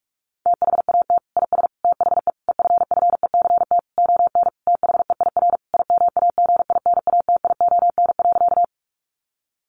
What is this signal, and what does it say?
Morse code, heard as a keyed tone, about 44 wpm, 700 Hz. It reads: THAT IS THE EFFECT ON THEIR IMAGINATION.
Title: Sentences from Top 1300 Words 45wpm